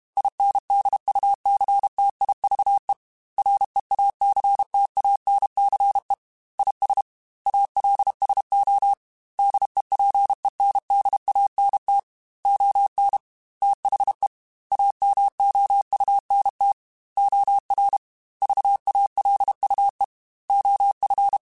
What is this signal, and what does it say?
Morse code, heard as a keyed tone, about 32 wpm, 800 Hz. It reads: INDUCTIVE REACTANCE IS ALSO DEPENDANT ON THE AMOUNT OR VALUE OF
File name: Technical Subjects / Technical Subjects-mp3s / XL @30